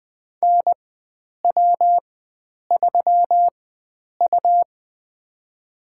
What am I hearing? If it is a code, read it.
Morse code, heard as a keyed tone, about 21 wpm, spaced as for 9 wpm, 700 Hz. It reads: NW3U
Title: USA Call Signs - 4 Characters - 4x Character Spacing - 20wpm